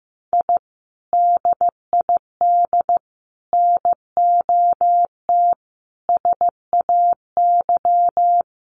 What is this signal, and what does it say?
Morse code, heard as a keyed tone, about 15 wpm, 700 Hz. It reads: I DID NOT SAY
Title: Sets of 5 Words Encore2 15wpm